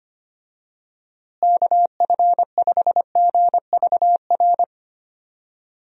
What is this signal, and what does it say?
Morse code, heard as a keyed tone, about 25 wpm, 700 Hz. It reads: KF5GVR
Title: USA Call Signs - 6 Characters - Repeated 3x - 25wpm